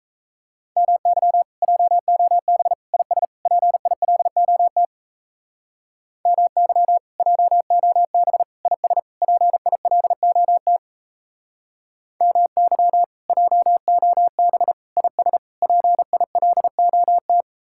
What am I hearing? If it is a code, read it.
Morse code, heard as a keyed tone, about 34 wpm, 700 Hz. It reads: MY JOB IS PILOT MY JOB IS PILOT MY JOB IS PILOT
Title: Speed-Racing Common QSO Phrases - 28wpm